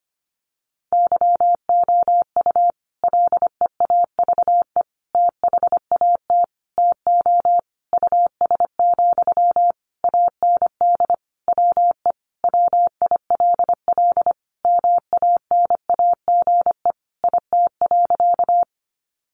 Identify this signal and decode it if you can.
Morse code, heard as a keyed tone, about 25 wpm, 700 Hz. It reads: YOU LEAVE THAT TO US, AND WE WILL MANAGE IT.
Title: Sentences from Top 1400 Words 25wpm